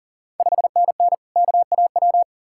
Morse code, heard as a keyed tone, about 40 wpm, 690 Hz. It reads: 5NN KAW
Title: Ontario QSO Party Exchange 40wpm